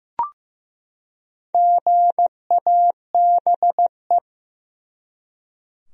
Morse code, heard as a keyed tone, about 15 wpm, 700 Hz. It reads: GABE
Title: USA Common Names - 15wpm